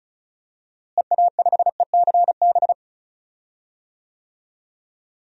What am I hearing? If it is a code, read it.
Morse code, heard as a keyed tone, about 36 wpm, 700 Hz. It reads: EA5ECB
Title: DXCC-100 Call Signs - 6 Characters - Repeated 3x - 35wpm